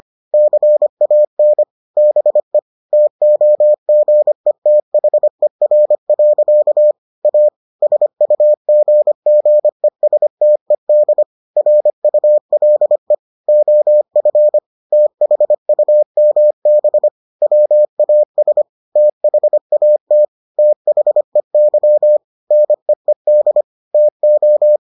Morse code, heard as a keyed tone, about 25 wpm, 600 Hz. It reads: CAN BE TOGETHER. A SUGGESTED RULE OF THUMB WAS THAT THEY NEED TO